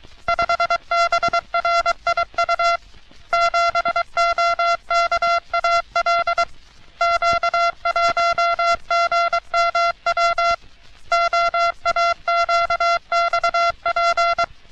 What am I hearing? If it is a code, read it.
Morse code, heard as a keyed tone, about 23 wpm, 680 Hz. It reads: 5BRIU 7OKAL Q1GMW OAQXP
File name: U.S. Navy - RN-2849-124 - Sixteens Morse Code Practice